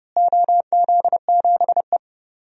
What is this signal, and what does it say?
Morse code, heard as a keyed tone, about 32 wpm, 700 Hz. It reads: OZ7E
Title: DXCC-100 Call Signs - 4 Characters - Repeated 3x - 30wpm